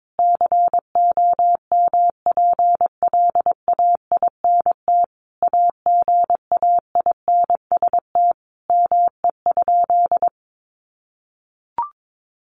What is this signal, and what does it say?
Morse code, heard as a keyed tone, about 22 wpm, 700 Hz. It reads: COMPLAINT AGAINST ME?
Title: Sentences from Top 1400 Words 22wpm